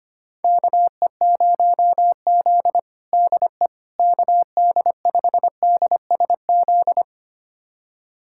Morse code, heard as a keyed tone, about 25 wpm, 700 Hz. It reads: KE0Z DE KD5DSZ
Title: USA Callsign DE Callsign 25wpm